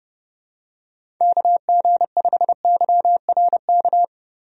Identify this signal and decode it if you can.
Morse code, heard as a keyed tone, about 30 wpm, 700 Hz. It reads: KG5YRK
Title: USA Call Signs - 6 Characters - Repeated 3x - 30wpm